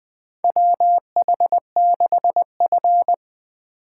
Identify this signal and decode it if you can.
Morse code, heard as a keyed tone, about 21 wpm, 700 Hz. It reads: WH6F
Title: USA Call Signs - 4 Characters - 3x Character Spacing - 20wpm